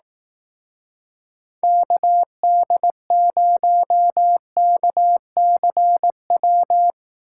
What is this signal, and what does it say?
Morse code, heard as a keyed tone, about 18 wpm, 700 Hz. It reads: KD0KCW
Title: Speed Racing - USA 2x3 Call Signs 15wpm